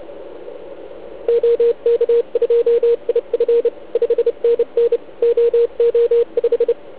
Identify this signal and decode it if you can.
Morse code, heard as a keyed tone, about 29 wpm, 470 Hz, with radio noise. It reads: OK2IF 5NN OO5
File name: Archiv / CUC149 / OK1IF CUC149